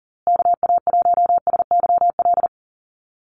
Morse code, heard as a keyed tone, about 40 wpm, 700 Hz. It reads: KA1SYL
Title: USA Call Signs Repeated 3x 40wpm